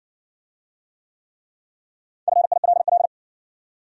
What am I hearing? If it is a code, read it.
Morse code, heard as a keyed tone, about 60 wpm, 700 Hz. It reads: FILL